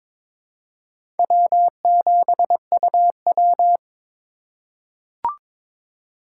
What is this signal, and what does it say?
Morse code, heard as a keyed tone, about 21 wpm, 700 Hz. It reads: W7UW
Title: USA Call Signs - 4 Characters - 3x Character Spacing - 22wpm